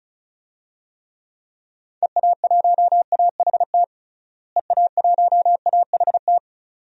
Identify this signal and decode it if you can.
Morse code, heard as a keyed tone, about 34 wpm, 700 Hz. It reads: EA1AHT EA1AHT
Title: DXCC-100 Call Signs - 6 Characters - Repeated 2x - 35wpm